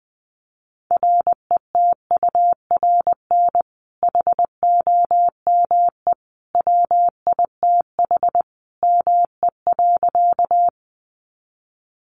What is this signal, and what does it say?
Morse code, heard as a keyed tone, about 20 wpm, 700 Hz. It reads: RETURN HOME WITH ME.
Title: Sentences from Top 800 Words 20wpm